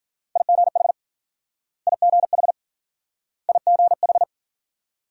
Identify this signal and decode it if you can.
Morse code, heard as a keyed tone, about 49 wpm, 700 Hz. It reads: IGH IGH IGH
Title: Speed-Racing Multi-letter Phonemes - 40wpm